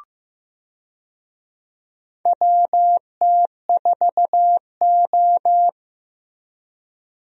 Morse code, heard as a keyed tone, about 15 wpm, 700 Hz. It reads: WT4O
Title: USA Call Signs - 4 Characters - Repeated 2x - 15wpm